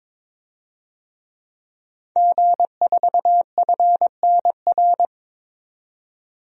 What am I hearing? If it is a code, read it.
Morse code, heard as a keyed tone, about 22 wpm, 700 Hz. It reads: G4FNR
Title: DXCC-100 Call Signs - 5 Characters - 22wpm